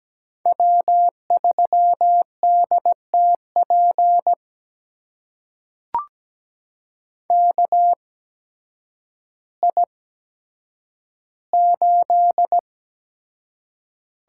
Morse code, heard as a keyed tone, about 17 wpm, spaced as for 9 wpm, 700 Hz. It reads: W3DTP KI8
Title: USA Call Signs - 5 Characters - 8x Character Spacing - 17wpm